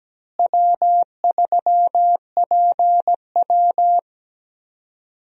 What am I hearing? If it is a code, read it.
Morse code, heard as a keyed tone, about 17 wpm, 700 Hz. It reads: W3PW
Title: USA Call Signs - 4 Characters - Repeated 3x - 17wpm